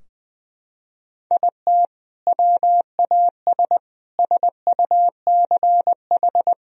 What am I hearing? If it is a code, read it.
Morse code, heard as a keyed tone, about 20 wpm, 700 Hz. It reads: IT WAS SUCH
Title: Sentences from Top 1000 Words - Review Part 4 - 20wpm